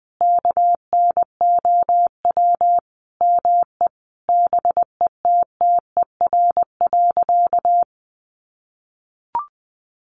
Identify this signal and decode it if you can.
Morse code, heard as a keyed tone, about 20 wpm, 700 Hz. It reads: KNOW ME BETTER.
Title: Sentences from Top 400 Words 20wpm